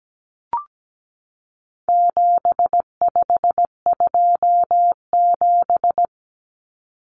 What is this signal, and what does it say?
Morse code, heard as a keyed tone, about 17 wpm, 700 Hz. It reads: 7527
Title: Numbers - 4-Digit 17wpm